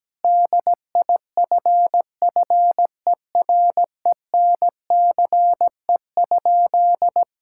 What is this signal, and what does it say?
Morse code, heard as a keyed tone, about 17 wpm, 700 Hz. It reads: DIFFERENCE?
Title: Sentences from Top 1000 Words - Review Part 5 - 17wpm